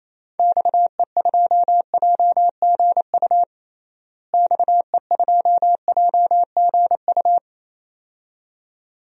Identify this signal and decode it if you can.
Morse code, heard as a keyed tone, about 28 wpm, 700 Hz. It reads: XE2JGU XE2JGU
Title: DXCC-100 Call Signs - 6 Characters - Repeated 2x - 28wpm